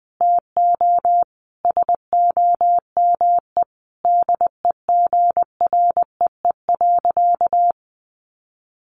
Morse code, heard as a keyed tone, about 20 wpm, 700 Hz. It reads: TO SOME DEGREE.
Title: Sentences from Top 1500 Words 20wpm